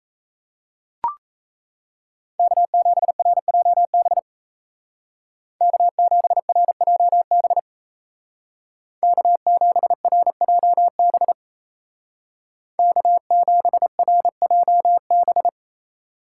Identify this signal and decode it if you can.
Morse code, heard as a keyed tone, about 37 wpm, 700 Hz. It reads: K7RJB K7RJB K7RJB K7RJB